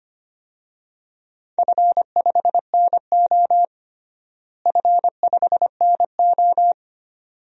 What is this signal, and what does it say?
Morse code, heard as a keyed tone, about 25 wpm, 700 Hz. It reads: F5NO F5NO
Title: DXCC-100 Call Signs - 4 Characters - Repeated 2x - 25wpm